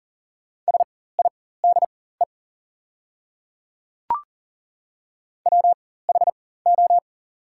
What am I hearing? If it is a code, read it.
Morse code, heard as a keyed tone, about 40 wpm, spaced as for 19 wpm, 690 Hz. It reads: SIDE WHO
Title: 4x Character Spacing - Top 100 Words - 40wpm